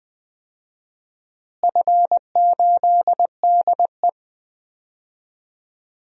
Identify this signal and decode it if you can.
Morse code, heard as a keyed tone, about 20 wpm, 700 Hz. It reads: F8DE
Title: DXCC-100 Call Signs - 4 Characters - Repeated 3x - 20wpm